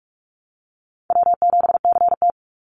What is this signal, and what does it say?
Morse code, heard as a keyed tone, about 46 wpm, 700 Hz. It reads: W7CT
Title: Call Signs - 45wpm